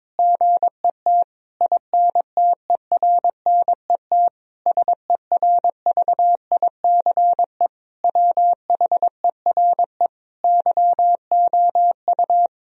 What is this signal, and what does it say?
Morse code, heard as a keyed tone, about 22 wpm, 700 Hz. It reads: GET INTERNET SERVICE WHERE YOU
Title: Sentences from Top 1000 Words - Review Part 4 - 22wpm